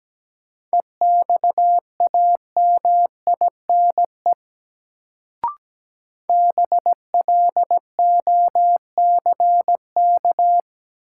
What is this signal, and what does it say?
Morse code, heard as a keyed tone, about 17 wpm, 700 Hz. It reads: EXAMINE BLOCK
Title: Sentences from Top 1600 Words - New Word Drill - 17wpm